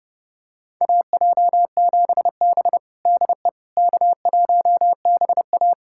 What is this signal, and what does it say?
Morse code, heard as a keyed tone, about 32 wpm, 700 Hz. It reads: AJ7B DE K1BA
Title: USA Callsign DE Callsign - 4 Characters - 30wpm